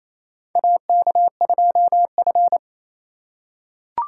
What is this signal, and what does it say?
Morse code, heard as a keyed tone, about 27 wpm, 700 Hz. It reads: AK2F
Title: USA Call Signs - 4 Characters - 4x Character Spacing - 28wpm